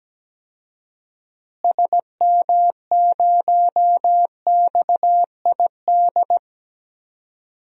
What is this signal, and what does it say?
Morse code, heard as a keyed tone, about 17 wpm, 700 Hz. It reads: SM0XID
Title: DXCC-100 Call Signs - 6 Characters - Repeated 2x - 17wpm